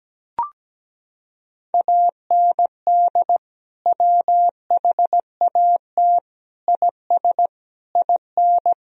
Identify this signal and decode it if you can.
Morse code, heard as a keyed tone, about 17 wpm, 700 Hz. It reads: AND WHAT IS IN
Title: Sentences from Top 1000 Words - Review Part 1 - 17wpm